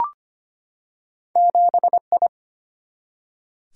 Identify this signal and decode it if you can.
Morse code, heard as a keyed tone, about 25 wpm, 690 Hz. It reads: 7I
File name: Two Letter-Number Combinations - 25wpm